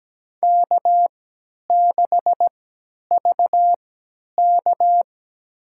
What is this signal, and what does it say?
Morse code, heard as a keyed tone, about 17 wpm, spaced as for 10 wpm, 700 Hz. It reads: K6VK
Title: USA Call Signs - 3x Character Spacing - 17wpm